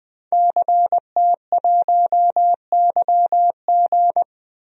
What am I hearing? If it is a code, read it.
Morse code, heard as a keyed tone, about 20 wpm, 700 Hz. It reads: CT1YG
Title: DXCC-100 Call Signs - 5 Characters - 20wpm